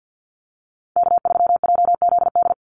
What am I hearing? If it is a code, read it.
Morse code, heard as a keyed tone, about 50 wpm, 700 Hz. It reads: K3JZD
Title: NA SOTA Chaser Callsigns 50wpm